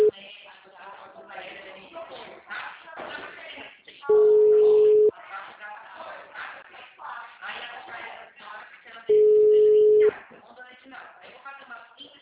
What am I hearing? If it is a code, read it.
Ringback tone (Europe, 425 Hz), with speech.